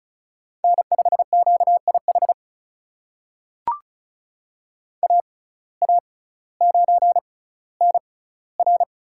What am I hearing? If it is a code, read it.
Morse code, heard as a keyed tone, about 34 wpm, spaced as for 18 wpm, 700 Hz. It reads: N5QIH AA9NR
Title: USA Call Signs - 5 Characters - 6x Character Spacing - 35wpm